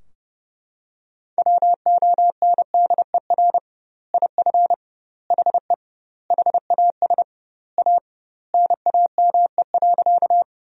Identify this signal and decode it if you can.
Morse code, heard as a keyed tone, about 30 wpm, 700 Hz. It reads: WONDER IF HE HAS A NAME.
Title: 2x Word-Spacing - Sentences from Top 500 Words - 30wpm